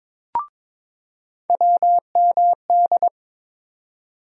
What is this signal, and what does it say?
Morse code, heard as a keyed tone, about 22 wpm, 700 Hz. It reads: WMD
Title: USA Call Sign Postfixes - 3 Letters 22wpm